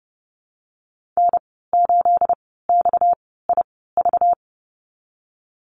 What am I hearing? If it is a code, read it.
Morse code, heard as a keyed tone, about 32 wpm, spaced as for 17 wpm, 700 Hz. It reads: N8XIV
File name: Lesson 37g - Letter x - Call Signs 30wpm